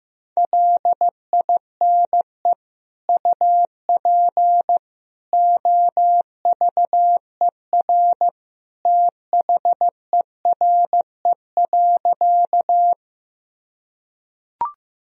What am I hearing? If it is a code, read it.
Morse code, heard as a keyed tone, about 15 wpm, 700 Hz. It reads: LINE UP OVER THERE.
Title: Sentences from Top 1000 Words - Review Part 1 - 15wpm